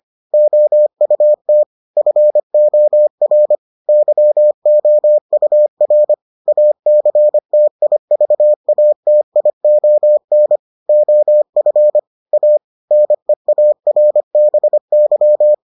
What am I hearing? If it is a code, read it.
Morse code, heard as a keyed tone, about 25 wpm, 600 Hz. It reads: OUT FOR YOUR ACTIVATION OF A NEARBY